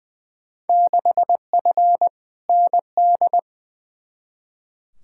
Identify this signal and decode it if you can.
Morse code, heard as a keyed tone, about 20 wpm, 700 Hz. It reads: 6F ND
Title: Field Day Exchanges Encore - 20wpm